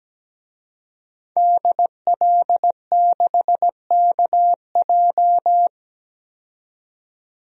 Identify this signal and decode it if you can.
Morse code, heard as a keyed tone, about 17 wpm, 700 Hz. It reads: DL6KJ
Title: DXCC-100 Call Signs - 5 Characters - 17wpm